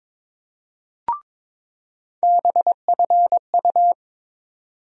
Morse code, heard as a keyed tone, about 22 wpm, 700 Hz. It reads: BFU
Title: USA Call Sign Postfixes - 3 Letters 22wpm